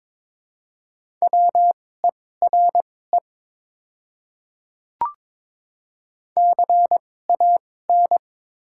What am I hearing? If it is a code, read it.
Morse code, heard as a keyed tone, about 22 wpm, spaced as for 16 wpm, 700 Hz. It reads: WERE CAN
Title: 2x Character Spacing - Top 100 Words - 22wpm